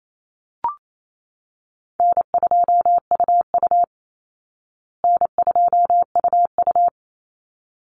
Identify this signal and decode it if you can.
Morse code, heard as a keyed tone, about 28 wpm, 700 Hz. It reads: N2UU N2UU
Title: Easy Call Signs Repeated 2x - 28wpm